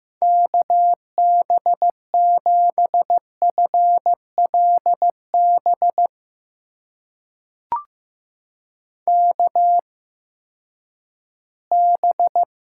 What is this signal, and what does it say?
Morse code, heard as a keyed tone, about 15 wpm, spaced as for 10 wpm, 700 Hz. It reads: KB7FLB KB
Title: USA Call Signs - 6 Characters - 8x Character Spacing - 15wpm